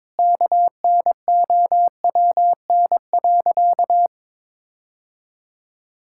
Morse code, heard as a keyed tone, about 22 wpm, 700 Hz. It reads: KNOWN.
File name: practice / 22wpm / Sentences with 3-Word Phrases - 22wpm